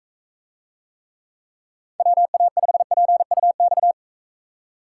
Morse code, heard as a keyed tone, about 42 wpm, 700 Hz. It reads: WA5PUX